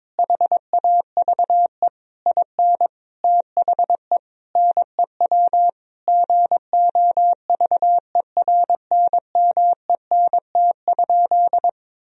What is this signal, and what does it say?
Morse code, heard as a keyed tone, about 22 wpm, 700 Hz. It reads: HAVE IN THE NEW GOVERNMENT?
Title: Sentences from Top 1000 Words - Review Part 3 - 22wpm